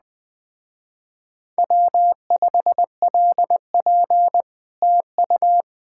Morse code, heard as a keyed tone, about 20 wpm, 700 Hz. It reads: W5LP TU
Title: Speed-Racing Common QSO Phrases - 17wpm